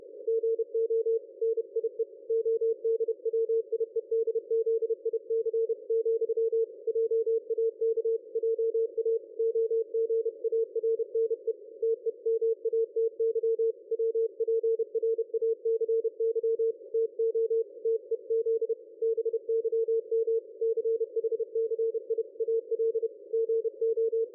Morse code, heard as a keyed tone, about 30 wpm, 450 Hz. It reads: GO NIE ODWIEDZIC, JAK JA OGARNE TEMATY WPRACY TO TEZ BYM CHCI AL GO
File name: dlazaawansowanych / nagraniazpasma / ExpertSDR2 line out 23.01.2020 14.22.53